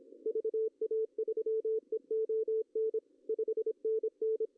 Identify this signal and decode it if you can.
Morse code, heard as a keyed tone, about 27 wpm, 410 Hz. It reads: VA3EON 5NN
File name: CW / cqww-cw-k5tr